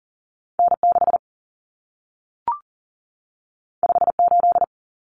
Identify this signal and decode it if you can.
Morse code, heard as a keyed tone, about 40 wpm, 700 Hz. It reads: N 6 5 8
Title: Common Cut Numbers - 2-Digit 40wpm